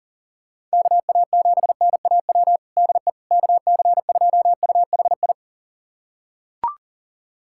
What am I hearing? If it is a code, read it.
Morse code, heard as a keyed tone, about 40 wpm, 700 Hz. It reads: KA7NAW DE KC2UHI
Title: USA Callsign DE Callsign 40wpm